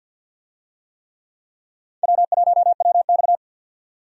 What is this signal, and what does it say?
Morse code, heard as a keyed tone, about 52 wpm, 700 Hz. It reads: W1WX
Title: USA Call Signs - 4 Characters - 50wpm